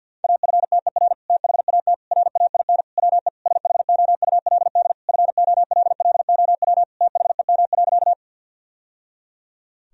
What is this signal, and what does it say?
Morse code, heard as a keyed tone, about 50 wpm, 700 Hz. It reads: AFTER THAT RAIN WE SHOULD FOLLOW THEM.
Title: Sentences from Top 1000 Words - Review Part 1 - 50wpm